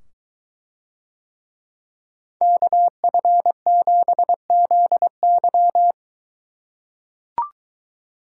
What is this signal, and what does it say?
Morse code, heard as a keyed tone, about 23 wpm, 700 Hz. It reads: KF7ZY